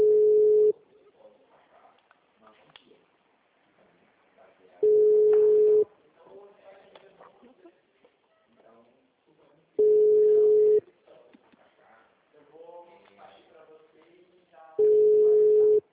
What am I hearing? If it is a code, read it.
Ringback tone (Europe, 425 Hz), with speech.
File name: out- -2011- - - .59956